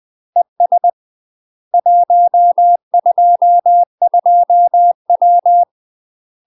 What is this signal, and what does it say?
Morse code, heard as a keyed tone, about 20 wpm, 700 Hz. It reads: ES 122W
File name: UK MASTER QSOs TRAINING / 20WPM / UK MASTER QSOs 0030